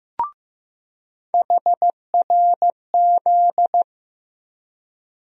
Morse code, heard as a keyed tone, about 15 wpm, 700 Hz. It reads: HRZ
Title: USA Call Sign Postfixes - 3 Letters 15wpm